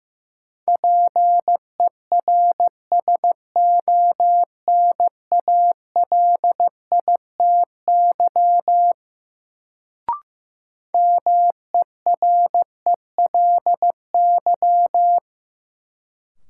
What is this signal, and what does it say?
Morse code, heard as a keyed tone, about 15 wpm, 700 Hz. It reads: PERSONALITY MERELY
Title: Sentences from Top 1400 Words - New Word Drill Alt - 15wpm